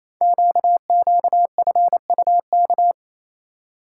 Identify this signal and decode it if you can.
Morse code, heard as a keyed tone, about 27 wpm, 700 Hz. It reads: QQFUK
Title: Code Groups - 5 Letters - 28wpm